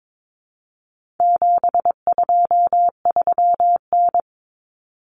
Morse code, heard as a keyed tone, about 22 wpm, 700 Hz. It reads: 723N
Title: Common Cut Numbers - 4-Digit 22wpm